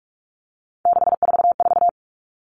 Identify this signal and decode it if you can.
Morse code, heard as a keyed tone, about 44 wpm, 700 Hz. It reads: 644
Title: Numbers - 3-Digit 45wpm